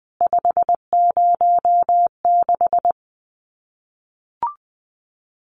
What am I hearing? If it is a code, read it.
Morse code, heard as a keyed tone, about 20 wpm, 700 Hz. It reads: 506
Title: CWT Exchange 20wpm